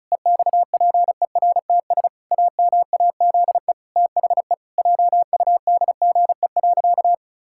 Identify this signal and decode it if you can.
Morse code, heard as a keyed tone, about 36 wpm, 700 Hz. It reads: EXPERTS AMAZE THE JUDGE.
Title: Pangram Challenge v2 - 35wpm